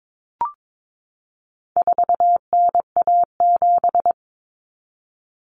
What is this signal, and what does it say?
Morse code, heard as a keyed tone, about 22 wpm, 700 Hz. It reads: 4NA7
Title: Common Cut Numbers - 4-Digit 22wpm